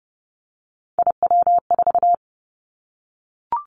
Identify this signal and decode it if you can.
Morse code, heard as a keyed tone, about 30 wpm, 700 Hz. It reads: IW4
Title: Callsign Prefixes for Top 25 DXCC - Repeated 3x - 30wpm